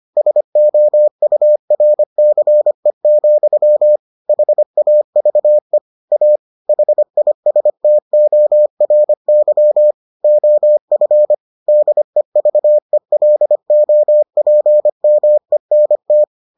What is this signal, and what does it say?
Morse code, heard as a keyed tone, about 25 wpm, 600 Hz. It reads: SOURCE, HAVE A HISTORY OF DEVELOPMENT